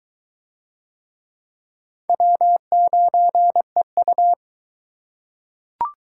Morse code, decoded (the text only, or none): W9EU